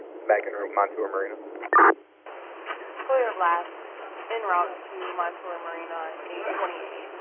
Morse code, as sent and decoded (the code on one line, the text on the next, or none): none